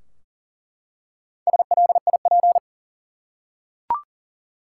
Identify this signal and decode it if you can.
Morse code, heard as a keyed tone, about 39 wpm, 700 Hz. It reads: SLIP
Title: Four Letter Words - 40wpm